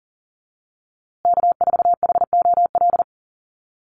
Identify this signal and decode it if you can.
Morse code, heard as a keyed tone, about 40 wpm, 700 Hz. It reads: K4HOL
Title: USA Call Signs Repeated 3x 40wpm